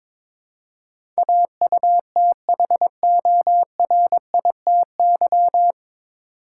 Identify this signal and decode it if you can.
Morse code, heard as a keyed tone, about 22 wpm, 700 Hz. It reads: AUTHORITY
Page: Morse Code Ninja practice 22wpm